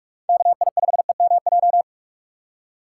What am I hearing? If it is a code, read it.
Morse code, heard as a keyed tone, about 46 wpm, 700 Hz. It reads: KI5EMJ
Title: USA Call Signs - 6 Characters - 2x Character Spacing - 45wpm